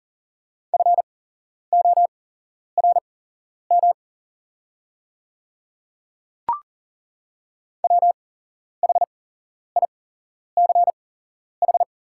Morse code, heard as a keyed tone, about 40 wpm, spaced as for 11 wpm, 700 Hz. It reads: FORM WHICH
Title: 8x Character Spacing - Top 200 Words - 40wpm